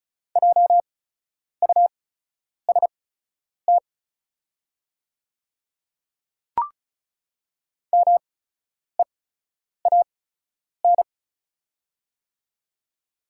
Morse code, heard as a keyed tone, about 36 wpm, spaced as for 10 wpm, 700 Hz. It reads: JUST MEAN
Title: 8x Character Spacing - Top 200 Words - 35wpm